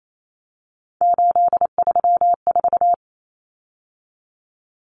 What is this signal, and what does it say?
Morse code, heard as a keyed tone, about 28 wpm, 700 Hz. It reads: 834
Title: Numbers - 3-Digit 28wpm